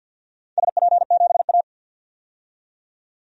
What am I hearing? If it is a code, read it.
Morse code, heard as a keyed tone, about 48 wpm, 700 Hz. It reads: SP7A
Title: DXCC-100 Call Signs - 4 Characters - 50wpm